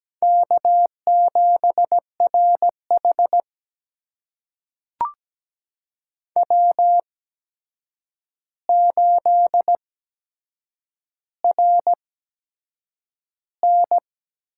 Morse code, heard as a keyed tone, about 17 wpm, spaced as for 7 wpm, 700 Hz. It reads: K7RH W8RN
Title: USA Call Signs - 4 Characters - 8x Character Spacing - 17wpm